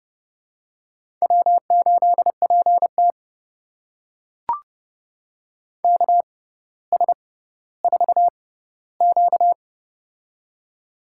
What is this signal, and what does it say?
Morse code, heard as a keyed tone, about 30 wpm, spaced as for 15 wpm, 700 Hz. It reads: W8PT KS4Q
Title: USA Call Signs - 4 Characters - 6x Character Spacing - 30wpm